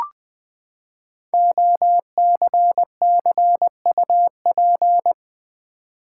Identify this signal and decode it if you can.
Morse code, heard as a keyed tone, about 20 wpm, 700 Hz. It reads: OCCUP
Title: Build A Word - 5 Letters At A Time - 10-Letter Words - 20wpm